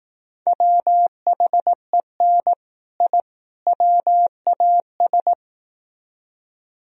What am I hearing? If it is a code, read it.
Morse code, heard as a keyed tone, about 18 wpm, 700 Hz. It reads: WHEN I WAS
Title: Speed-Racing - Sets of 3 Words 15wpm